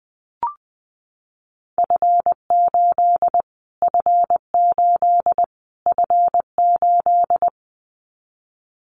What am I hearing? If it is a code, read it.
Morse code, heard as a keyed tone, about 20 wpm, 700 Hz. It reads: F8 F8 F8
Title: Callsign Prefixes for Top 25 DXCC - Repeated 3x - 20wpm